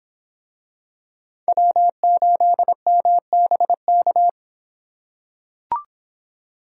Morse code, heard as a keyed tone, about 23 wpm, 700 Hz. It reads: W8MBK